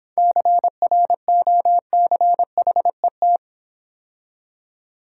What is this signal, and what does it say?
Morse code, heard as a keyed tone, about 26 wpm, 700 Hz. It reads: CROCHET